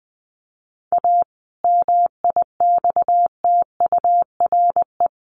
Morse code, heard as a keyed tone, about 19 wpm, 700 Hz. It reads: A MIXTURE
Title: Sentences from Top 1500 Words 20wpm